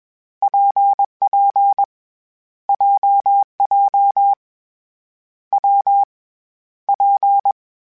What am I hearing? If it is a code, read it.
Morse code, heard as a keyed tone, about 21 wpm, 800 Hz. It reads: PP JJ W P
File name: Variable Speed / Variable Speed-mp3s / WPJ WPJ1 sp @20